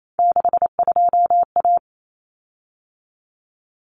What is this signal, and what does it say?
Morse code, heard as a keyed tone, about 28 wpm, 700 Hz. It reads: 62A
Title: Common Cut Numbers - 3-Digit 28wpm